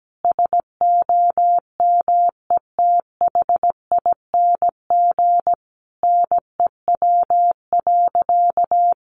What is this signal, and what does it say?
Morse code, heard as a keyed tone, about 17 wpm, 700 Hz. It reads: SOMETHING NEW.
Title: Sentences from Top 1100 Words 17wpm